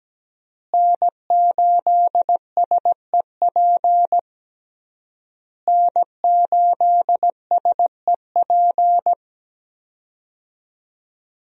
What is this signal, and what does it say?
Morse code, heard as a keyed tone, about 17 wpm, 700 Hz. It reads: N8SEP N8SEP
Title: USA Call Signs - 5 Characters - Repeated 3x - 17wpm